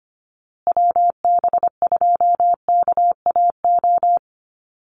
Morse code, heard as a keyed tone, about 25 wpm, 700 Hz. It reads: WB2KAO
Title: Call Signs Repeated 3x - 25wpm